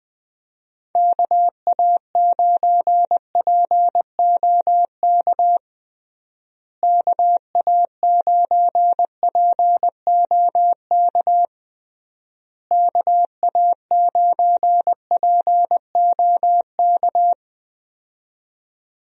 Morse code, heard as a keyed tone, about 20 wpm, 700 Hz. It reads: KA9POK KA9POK KA9POK
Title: USA Call Signs - 6 Characters - Repeated 3x - 20wpm